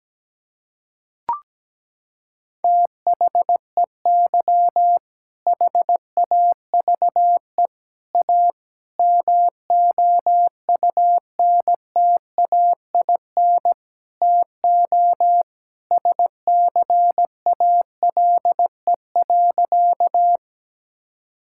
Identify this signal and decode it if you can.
Morse code, heard as a keyed tone, about 17 wpm, 700 Hz. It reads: THEY HAVE A MOUNTAIN TO SCALE.
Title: Sentences from Top 1000 Words - Review Part 5 - 17wpm